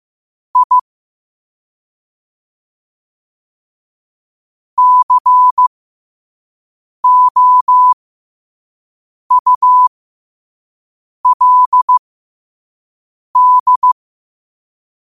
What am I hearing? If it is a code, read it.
Morse code, heard as a keyed tone, about 15 wpm, spaced as for 5 wpm, 1000 Hz. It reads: I COULD
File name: qotdfiles / 5wpm